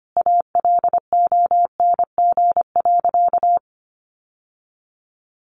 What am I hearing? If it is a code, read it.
Morse code, heard as a keyed tone, about 25 wpm, 700 Hz. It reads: ALONG.
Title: Sentences from Top 1500 Words 25wpm